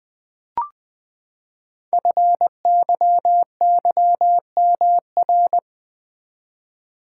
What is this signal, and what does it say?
Morse code, heard as a keyed tone, about 20 wpm, 700 Hz. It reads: FYYMR
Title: Code Groups - 5 Letter:Numbers - 20wpm